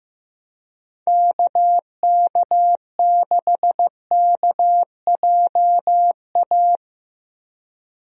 Morse code, heard as a keyed tone, about 15 wpm, 700 Hz. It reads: KK6KJA
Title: USA Call Signs - 6 Characters - Repeated 3x - 15wpm